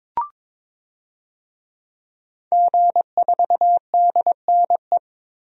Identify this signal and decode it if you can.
Morse code, heard as a keyed tone, about 22 wpm, 700 Hz. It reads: G4DNE
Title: DXCC-100 Call Signs - 5 Characters - Repeated 3x - 22wpm